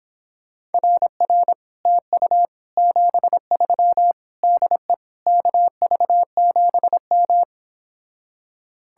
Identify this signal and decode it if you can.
Morse code, heard as a keyed tone, about 26 wpm, 700 Hz. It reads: RR TU 73 DE KV7M
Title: Speed-Racing Common QSO Phrases Encore - 22wpm